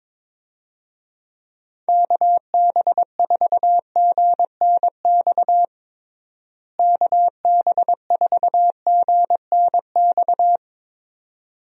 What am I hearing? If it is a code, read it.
Morse code, heard as a keyed tone, about 22 wpm, 700 Hz. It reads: KB4GNX KB4GNX
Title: USA Call Signs - 6 Characters - Repeated 2x - 22wpm